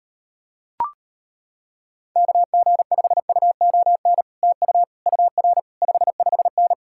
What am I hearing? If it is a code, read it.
Morse code, heard as a keyed tone, about 33 wpm, 700 Hz. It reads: KG5UON TU UR 55N